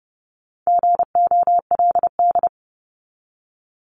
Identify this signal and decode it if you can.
Morse code, heard as a keyed tone, about 30 wpm, 700 Hz. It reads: GOLD
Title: Top 500 Words 30wpm